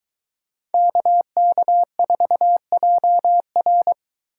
Morse code, heard as a keyed tone, about 22 wpm, 700 Hz. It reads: KK4JR